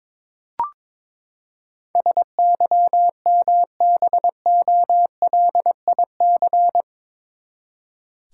Morse code, heard as a keyed tone, about 21 wpm, 700 Hz. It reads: SYMBOLIC
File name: practice / 22wpm / Eight Letter Words - 22wpm